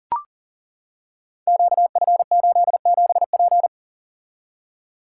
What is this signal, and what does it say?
Morse code, heard as a keyed tone, about 40 wpm, 700 Hz. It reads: QF87P
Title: Code Groups - 5 Letter:Numbers - 40wpm